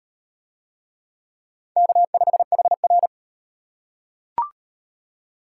Morse code, heard as a keyed tone, about 38 wpm, 700 Hz. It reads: K5HR